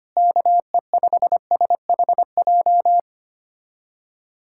Morse code, heard as a keyed tone, about 25 wpm, 700 Hz. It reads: KE5SHJ
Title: USA Callsign DE Callsign 25wpm